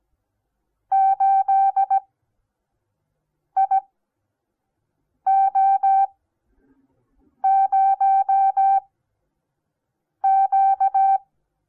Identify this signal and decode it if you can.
Morse code, heard as a keyed tone, about 17 wpm, spaced as for 5 wpm, 780 Hz. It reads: 8IO0Q
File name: Onlinelehrgang / morsen / DJ4UF Morsen08